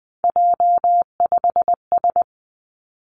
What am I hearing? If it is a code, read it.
Morse code, heard as a keyed tone, about 20 wpm, 700 Hz. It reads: J5S
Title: Three Challenging Letter-Number Combinations 20wpm